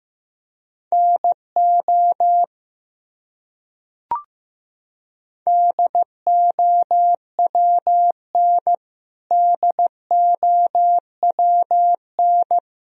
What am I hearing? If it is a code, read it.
Morse code, heard as a keyed tone, about 15 wpm, 700 Hz. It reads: NO DOWN DOWN
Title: Top 100 Words Repeated 15wpm